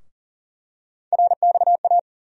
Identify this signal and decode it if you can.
Morse code, heard as a keyed tone, about 40 wpm, 700 Hz. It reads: RXA
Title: USA Call Sign Postfixes - 3 Letters 40wpm